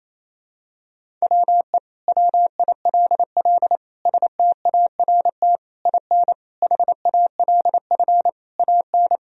Morse code, heard as a keyed tone, about 28 wpm, 700 Hz. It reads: WE WILL START IN HALF AN
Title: Sentences from Top 1000 Words - Review Part 1 - 28wpm